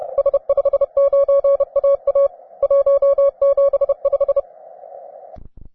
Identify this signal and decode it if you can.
Morse code, heard as a keyed tone, about 31 wpm, 560 Hz. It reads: S59AA 175